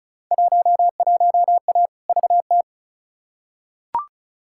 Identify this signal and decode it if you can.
Morse code, heard as a keyed tone, about 36 wpm, 700 Hz. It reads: 11A VT
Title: Field Day Exchanges Encore - 35wpm